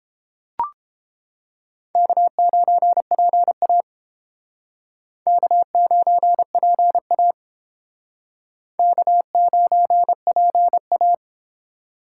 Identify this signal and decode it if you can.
Morse code, heard as a keyed tone, about 30 wpm, 700 Hz. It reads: K9PA K9PA K9PA